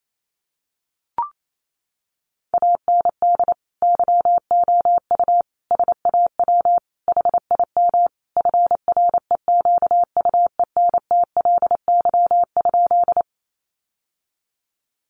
Morse code, heard as a keyed tone, about 28 wpm, 700 Hz. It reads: AND YOU SAW HIM FREQUENTLY?
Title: Sentences from Top 1300 Words 28wpm